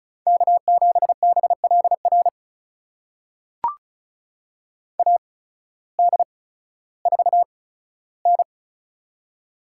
Morse code, heard as a keyed tone, about 36 wpm, spaced as for 16 wpm, 700 Hz. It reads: K7BLR AD4N
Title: USA Call Signs - 8x Character Spacing - 35wpm